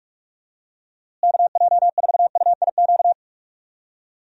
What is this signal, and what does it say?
Morse code, heard as a keyed tone, about 43 wpm, 700 Hz. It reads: KJ4UIQ
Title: USA Call Signs - 6 Characters - Repeated 2x - 45wpm